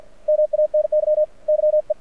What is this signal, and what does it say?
Morse code, heard as a keyed tone, about 54 wpm, 610 Hz. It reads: MANY YE
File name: archived / CRSnet